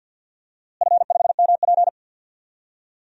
Morse code, heard as a keyed tone, about 47 wpm, 700 Hz. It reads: F5MP
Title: DXCC-100 Call Signs - 4 Characters - Repeated 2x - 50wpm